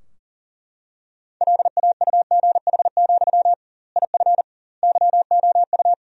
Morse code, heard as a keyed tone, about 40 wpm, 700 Hz. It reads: LAUGH, IF YOU
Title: 2x Word-Spacing - Sentences from Top 500 Words - 40wpm